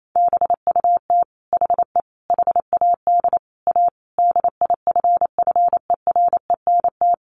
Morse code, heard as a keyed tone, about 28 wpm, 700 Hz. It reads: BUT HE HAD A DIFFERENT
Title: Sentences from Top 1200 Words 28wpm